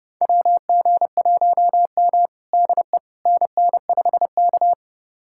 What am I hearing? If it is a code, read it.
Morse code, heard as a keyed tone, about 30 wpm, 700 Hz. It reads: WG1M DE NN5K
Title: USA Callsign DE Callsign - 4 Characters - 30wpm